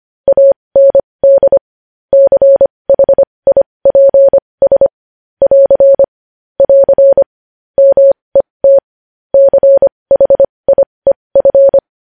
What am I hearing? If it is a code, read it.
Morse code, heard as a keyed tone, about 25 wpm, spaced as for 20 wpm, 550 Hz. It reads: AND CHIPS <AR> <AR> MET CHIEF
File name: podcast / News headlines-1x-25x20